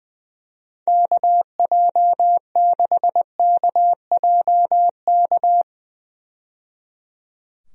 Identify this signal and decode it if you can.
Morse code, heard as a keyed tone, about 20 wpm, 700 Hz. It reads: KJ6KJK
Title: USA Call Signs - 6 Characters - 20wpm